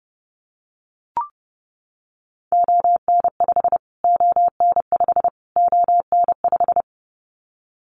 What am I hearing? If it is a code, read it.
Morse code, heard as a keyed tone, about 30 wpm, 700 Hz. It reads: ON5 ON5 ON5
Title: Callsign Prefixes for Top 25 DXCC - Repeated 3x - 30wpm